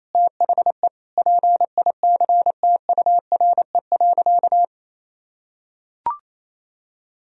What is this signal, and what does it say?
Morse code, heard as a keyed tone, about 28 wpm, 700 Hz. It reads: THE PICTURE.
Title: Sentences from Top 1000 Words - Review Part 2 - 28wpm